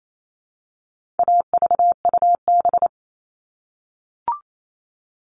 Morse code, heard as a keyed tone, about 28 wpm, 700 Hz. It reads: AVUB
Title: All Cut Numbers - 4-Digit 28wpm